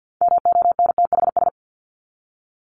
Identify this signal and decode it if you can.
Morse code, heard as a keyed tone, about 50 wpm, 700 Hz. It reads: MONTHS
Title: ES:S Suffix - Top 100 Words with Base Words 50wpm